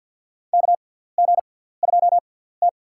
Morse code, heard as a keyed tone, about 50 wpm, spaced as for 17 wpm, 700 Hz. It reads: KG2T
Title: USA Call Signs - 6x Character Spacing - 50wpm